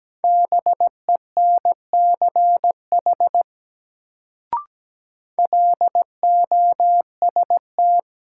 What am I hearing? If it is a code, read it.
Morse code, heard as a keyed tone, about 17 wpm, 700 Hz. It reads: BENCH LOST
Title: Sentences from Top 1400 Words - New Word Drill Alt - 17wpm